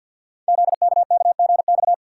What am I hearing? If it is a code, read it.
Morse code, heard as a keyed tone, about 50 wpm, 700 Hz. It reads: ZKKGX
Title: Code Groups - 5 Letter:Numbers - 50wpm